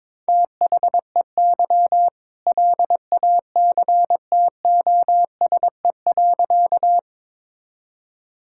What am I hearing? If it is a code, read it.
Morse code, heard as a keyed tone, about 22 wpm, 700 Hz. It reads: THEY LACTOSE.
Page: morse-code-ninja-practice-22wpm